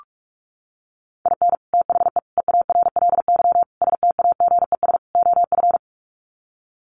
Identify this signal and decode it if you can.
Morse code, heard as a keyed tone, about 44 wpm, 700 Hz. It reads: IN THE EARLY STAGES OF
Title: Sets of 5 Words Encore 45wpm